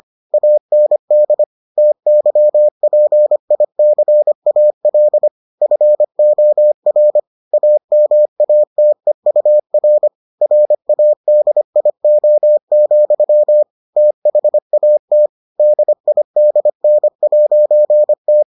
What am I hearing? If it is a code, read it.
Morse code, heard as a keyed tone, about 25 wpm, 600 Hz. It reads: AND TYPICAL FOR AMATEUR RADIO, THAT DIDN'T